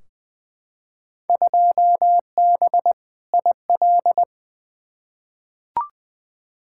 Morse code, heard as a keyed tone, about 20 wpm, 700 Hz. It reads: 2B IL
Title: Field Day Exchanges Encore - 20wpm